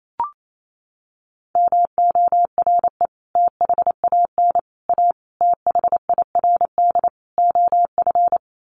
Morse code, heard as a keyed tone, about 28 wpm, 700 Hz. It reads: MORE THAN A THIRD OF
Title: Sets of 5 Words Encore2 28wpm